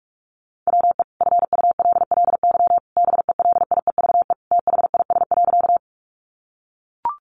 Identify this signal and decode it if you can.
Morse code, heard as a keyed tone, about 44 wpm, 700 Hz. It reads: WE FULLY BELIEVE THIS.
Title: Sentences from Top 1300 Words 45wpm